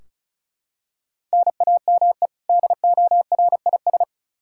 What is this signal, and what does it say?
Morse code, heard as a keyed tone, about 36 wpm, 700 Hz. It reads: NAME DORIS
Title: Practice QSO - Exchange 1 - Beginning - 35wpm